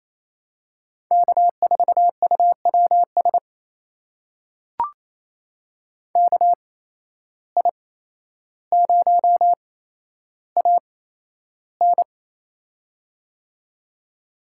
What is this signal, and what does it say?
Morse code, heard as a keyed tone, about 28 wpm, spaced as for 12 wpm, 700 Hz. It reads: K4UWS KI0AN
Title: USA Call Signs - 8x Character Spacing - 28wpm